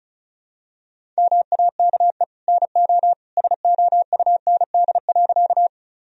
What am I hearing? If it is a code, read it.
Morse code, heard as a keyed tone, about 36 wpm, 700 Hz. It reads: MAKE NO SOUND.
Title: Sentences from Top 1000 Words - Review Part 1 - 35wpm